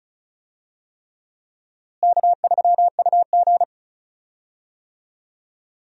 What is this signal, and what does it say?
Morse code, heard as a keyed tone, about 36 wpm, 700 Hz. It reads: K3UG
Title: USA Call Signs - 4 Characters - Repeated 3x - 35wpm